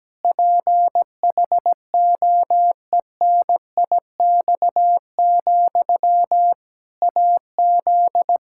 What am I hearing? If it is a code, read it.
Morse code, heard as a keyed tone, about 17 wpm, 700 Hz. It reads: PHOENIX, AZ
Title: Practice QSO - Exchange 1 - Beginning - 17wpm